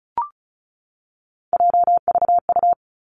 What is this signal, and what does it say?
Morse code, heard as a keyed tone, about 36 wpm, 700 Hz. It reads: JVU
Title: Three Challenging Letter-Number Combinations 35wpm